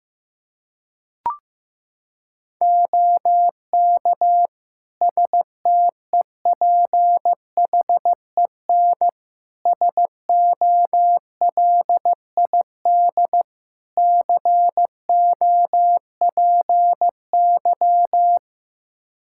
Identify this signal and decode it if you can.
Morse code, heard as a keyed tone, about 15 wpm, 700 Hz. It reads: OK STEPHEN SOLID COPY
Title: K1USN SST QSO Practice 15wpm